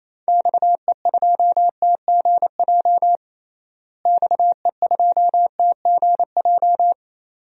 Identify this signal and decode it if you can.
Morse code, heard as a keyed tone, about 28 wpm, 700 Hz. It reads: XE2TGJ XE2TGJ
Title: DXCC-100 Call Signs - 6 Characters - Repeated 2x - 28wpm